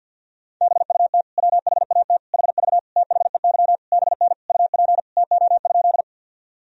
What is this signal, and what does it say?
Morse code, heard as a keyed tone, about 52 wpm, 700 Hz. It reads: BUT WHAT HV THEY BN UP TO?
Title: Sentences from Top 100 Words and Abbreviations 50wpm